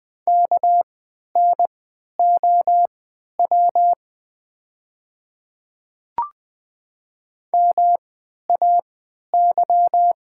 Morse code, heard as a keyed tone, about 20 wpm, spaced as for 11 wpm, 700 Hz. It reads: KNOW MAY
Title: 3x Character Spacing - Top 200 Words - 20wpm